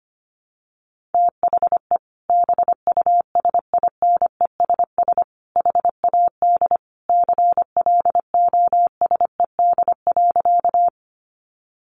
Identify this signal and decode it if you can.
Morse code, heard as a keyed tone, about 25 wpm, 700 Hz. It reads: THE BUSINESS HAD CLOSED.
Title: Sentences from Top 1400 Words 25wpm